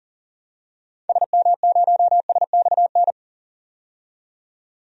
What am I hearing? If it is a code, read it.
Morse code, heard as a keyed tone, about 40 wpm, 700 Hz. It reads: SM0SXN
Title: DXCC-100 Call Signs - 6 Characters - 40wpm